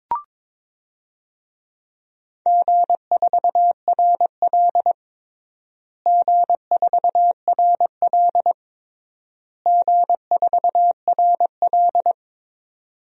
Morse code, heard as a keyed tone, about 22 wpm, 700 Hz. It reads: G4RL G4RL G4RL
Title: DXCC-100 Call Signs - 4 Characters - Repeated 3x - 22wpm